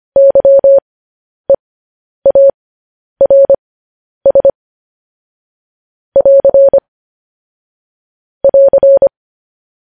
Morse code, heard as a keyed tone, about 25 wpm, spaced as for 10 wpm, 550 Hz. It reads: YEARS <AR> <AR>
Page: news headlines in morse code at 10 wpm